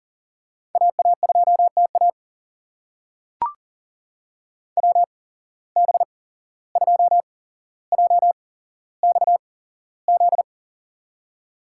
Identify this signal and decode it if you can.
Morse code, heard as a keyed tone, about 40 wpm, spaced as for 16 wpm, 700 Hz. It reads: AA2TA WB2JXZ
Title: USA Call Signs - 8x Character Spacing - 40wpm